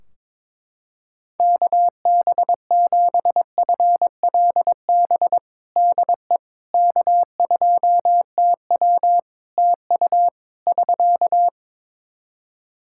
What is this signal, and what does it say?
Morse code, heard as a keyed tone, about 22 wpm, 700 Hz. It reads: KB7FLB DE K2TW TU <SK>
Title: Practice QSO - Exchange 4 - Ending - 22wpm